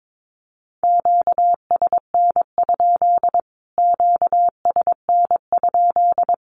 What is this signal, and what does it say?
Morse code, heard as a keyed tone, about 22 wpm, 700 Hz. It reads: QSN? QSN?
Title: Q-Codes Repeated 3x 22wpm